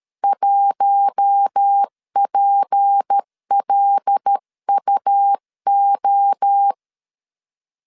Morse code, heard as a keyed tone, about 13 wpm, 800 Hz. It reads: 1PLUO
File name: CW / DET / 1 6 1500C S C 12wpm 3533